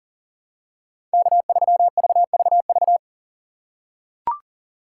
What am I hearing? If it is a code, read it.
Morse code, heard as a keyed tone, about 40 wpm, 700 Hz. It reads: K3VVV
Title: USA Call Signs - 5 Characters - 4x Character Spacing - 40wpm